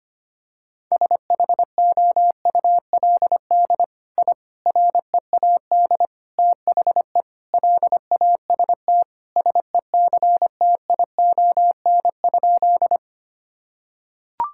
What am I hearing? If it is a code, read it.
Morse code, heard as a keyed tone, about 25 wpm, 700 Hz. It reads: SHOULD I READ THE LAST SECTION?
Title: Sentences from Top 1000 Words - Review Part 5 - 25wpm